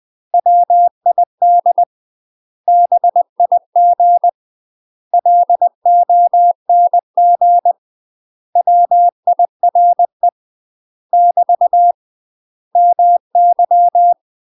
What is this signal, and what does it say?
Morse code, heard as a keyed tone, about 20 wpm, 700 Hz. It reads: WID BIG LONG WIRE <BT> MY
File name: UK MASTER QSOs TRAINING/20WPM